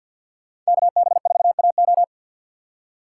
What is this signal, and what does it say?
Morse code, heard as a keyed tone, about 63 wpm, 700 Hz. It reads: KB4AO